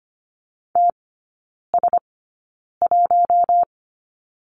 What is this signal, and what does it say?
Morse code, heard as a keyed tone, about 25 wpm, spaced as for 9 wpm, 690 Hz. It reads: TS1
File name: Lesson 15e - Number 5 - Three Characters 25wpm